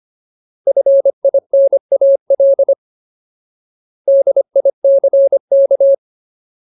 Morse code, heard as a keyed tone, about 25 wpm, 550 Hz. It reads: FINAL DICK
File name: Ultimate Word List / 25WPM Ultimate